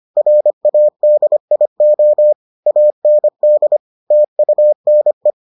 Morse code, heard as a keyed tone, about 25 wpm, 600 Hz. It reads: RADIO AND TUNE